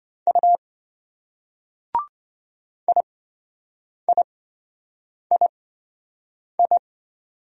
Morse code, heard as a keyed tone, about 27 wpm, spaced as for 7 wpm, 700 Hz. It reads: U IIII